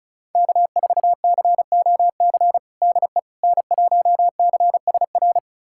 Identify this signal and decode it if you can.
Morse code, heard as a keyed tone, about 36 wpm, 700 Hz. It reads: K4COC DE N1CSR
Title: USA Callsign DE Callsign 35wpm